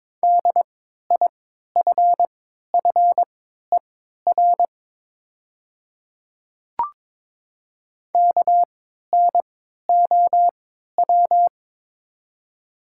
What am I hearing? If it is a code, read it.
Morse code, heard as a keyed tone, about 22 wpm, spaced as for 12 wpm, 700 Hz. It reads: DIFFER KNOW
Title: 3x Character Spacing - Top 300 Words - 22wpm